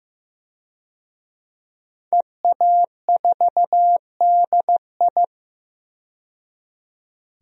Morse code, heard as a keyed tone, about 15 wpm, 700 Hz. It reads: EA4DI
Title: DXCC-100 Call Signs - 5 Characters - 15wpm